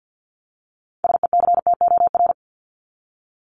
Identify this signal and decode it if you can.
Morse code, heard as a keyed tone, about 50 wpm, 700 Hz. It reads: SECTOR
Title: Top 1751-2000 Words 50wpm